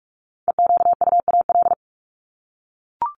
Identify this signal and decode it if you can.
Morse code, heard as a keyed tone, about 44 wpm, 700 Hz. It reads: EQUAL
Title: Top 1251-1500 Words 45wpm